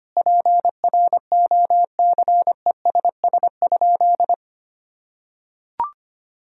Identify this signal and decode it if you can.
Morse code, heard as a keyed tone, about 25 wpm, 700 Hz. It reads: PROCESS?
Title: Sentences from Top 1000 Words - Review Part 3 - 25wpm